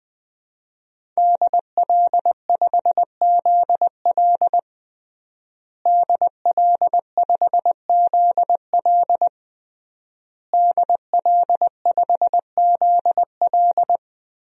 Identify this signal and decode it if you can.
Morse code, heard as a keyed tone, about 20 wpm, 700 Hz. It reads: DL5ZL DL5ZL DL5ZL
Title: DXCC-100 Call Signs - 5 Characters - Repeated 3x - 20wpm